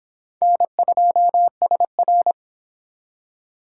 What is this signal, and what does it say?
Morse code, heard as a keyed tone, about 25 wpm, 700 Hz. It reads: N2SR